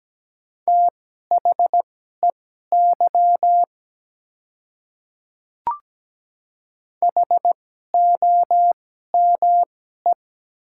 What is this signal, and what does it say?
Morse code, heard as a keyed tone, about 17 wpm, spaced as for 12 wpm, 700 Hz. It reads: THEY HOME
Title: 2x Character Spacing - Top 200 Words - 17wpm